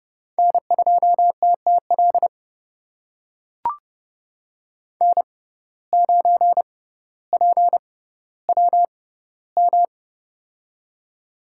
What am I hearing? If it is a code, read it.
Morse code, heard as a keyed tone, about 30 wpm, spaced as for 15 wpm, 700 Hz. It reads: N2TTL N9PWM
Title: USA Call Signs - 5 Characters - 6x Character Spacing - 30wpm